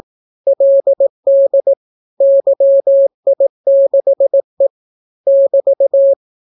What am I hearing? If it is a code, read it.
Morse code, heard as a keyed tone, about 18 wpm, 550 Hz. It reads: LD YI6E <BT>